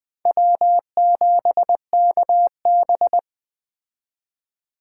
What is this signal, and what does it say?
Morse code, heard as a keyed tone, about 21 wpm, 700 Hz. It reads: W7KB